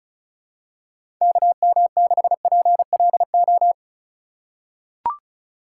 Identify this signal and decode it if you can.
Morse code, heard as a keyed tone, about 36 wpm, 700 Hz. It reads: KM6PLO
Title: USA Call Signs - 6 Characters - 3x Character Spacing - 35wpm